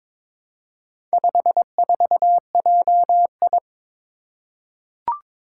Morse code, heard as a keyed tone, about 23 wpm, 690 Hz. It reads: 54JI
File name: practice / 22wpm / Four Letter-Number Combinations - 22wpm